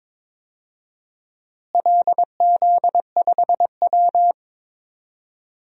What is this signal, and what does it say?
Morse code, heard as a keyed tone, about 21 wpm, 700 Hz. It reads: LZ5W
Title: DXCC-100 Call Signs - 4 Characters - Repeated 3x - 22wpm